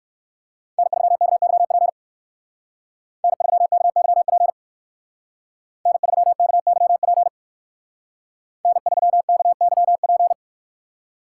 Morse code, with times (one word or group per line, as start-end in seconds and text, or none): 0.8-1.9 N3KYP
3.2-4.5 N3KYP
5.9-7.3 N3KYP
8.6-10.3 N3KYP